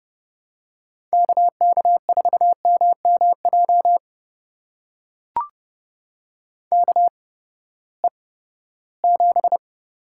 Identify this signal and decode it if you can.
Morse code, heard as a keyed tone, about 30 wpm, spaced as for 17 wpm, 700 Hz. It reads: KK4MMJ KE7
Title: USA Call Signs - 6 Characters - 8x Character Spacing - 30wpm